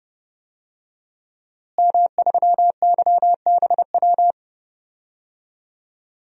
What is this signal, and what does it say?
Morse code, heard as a keyed tone, about 31 wpm, 700 Hz. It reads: M3YBW
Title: DXCC-100 Call Signs - 5 Characters - Repeated 3x - 30wpm